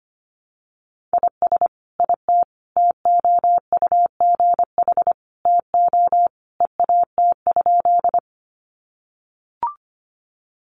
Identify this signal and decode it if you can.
Morse code, heard as a keyed tone, about 25 wpm, 700 Hz. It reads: IS IT TOUGH TO EAT?
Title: Sentences from Top 1200 Words 25wpm